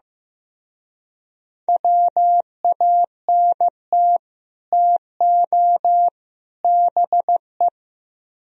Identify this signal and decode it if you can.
Morse code, heard as a keyed tone, about 18 wpm, 700 Hz. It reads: WANT TO BE
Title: Speed-Racing - Sets of 3 Words 15wpm